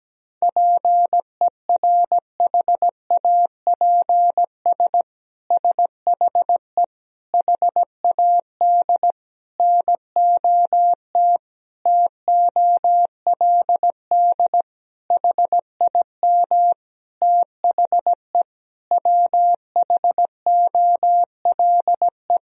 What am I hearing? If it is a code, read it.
Morse code, heard as a keyed tone, about 17 wpm, 700 Hz. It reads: PERHAPS SHE HAD NOT TOLD HIM THE WHOLE
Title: Sentences from Top 1000 Words - Review Part 1 - 17wpm